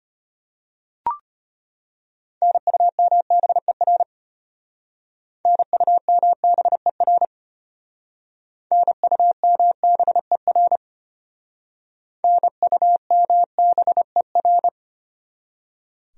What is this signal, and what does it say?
Morse code, heard as a keyed tone, about 32 wpm, 700 Hz. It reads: NUMBER NUMBER NUMBER NUMBER